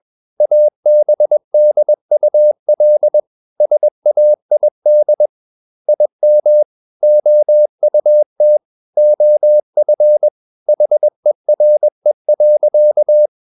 Morse code, heard as a keyed tone, about 21 wpm, 600 Hz. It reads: ABDUL SAID IM OUT OF HERE.
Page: CW med Gnister